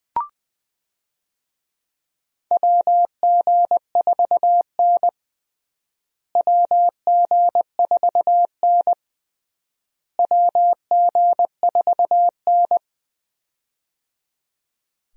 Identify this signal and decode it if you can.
Morse code, heard as a keyed tone, about 20 wpm, 700 Hz. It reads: WG4N WG4N WG4N
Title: USA Call Signs - 4 Characters - Repeated 3x - 20wpm